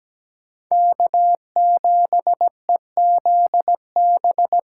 Morse code, heard as a keyed tone, about 17 wpm, 700 Hz. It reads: K7EZB
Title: USA Callsign DE Callsign - 5 Characters - 17wpm